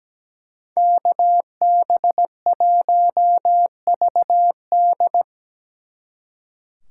Morse code, heard as a keyed tone, about 20 wpm, 700 Hz. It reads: KB1VD
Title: Speed Racing - USA 2x2 Call Signs 17wpm